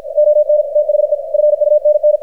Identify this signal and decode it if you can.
Morse code, heard as a keyed tone, about 53 wpm, 600 Hz, with radio noise. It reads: ON THE FUNN
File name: archived / CRSnet